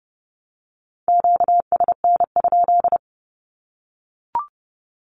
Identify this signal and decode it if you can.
Morse code, heard as a keyed tone, about 30 wpm, 700 Hz. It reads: QSN?
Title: Q-Codes Repeated 3x 30wpm